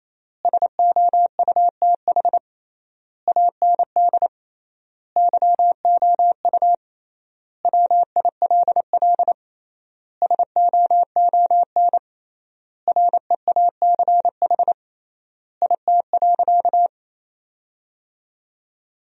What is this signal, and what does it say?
Morse code, heard as a keyed tone, about 28 wpm, 700 Hz. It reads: SOUTH AND YOU WILL SOON REACH IT.
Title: 3x Word-Spacing - Sentences from Top 400 Words - 28wpm